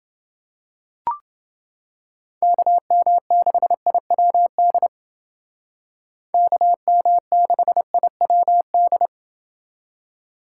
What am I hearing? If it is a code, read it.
Morse code, heard as a keyed tone, about 29 wpm, 700 Hz. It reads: KM6IWD KM6IWD